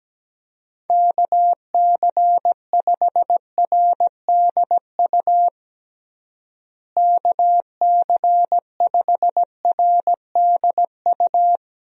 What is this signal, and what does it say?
Morse code, heard as a keyed tone, about 17 wpm, 700 Hz. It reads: KC5RDU KC5RDU
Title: USA Call Signs - 6 Characters - Repeated 3x - 17wpm